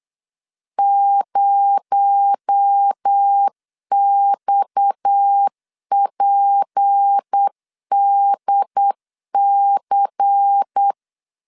Morse code, heard as a keyed tone, about 8 wpm, 800 Hz. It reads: 0XPDC
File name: CW / DET / 1 9 1500C S A 8wpm 5637